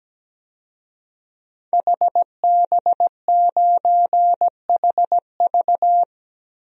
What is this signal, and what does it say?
Morse code, heard as a keyed tone, about 17 wpm, 700 Hz. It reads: HB9HV
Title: DXCC-100 Call Signs - 5 Characters - 17wpm